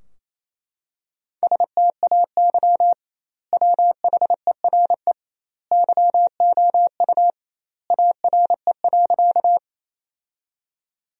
Morse code, heard as a keyed tone, about 28 wpm, 700 Hz. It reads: STAY WHERE YOU ARE.
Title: 2x Word-Spacing - Sentences from Top 500 Words - 28wpm